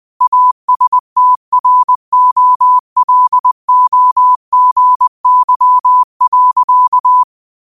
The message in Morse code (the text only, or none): ASTROLOGY.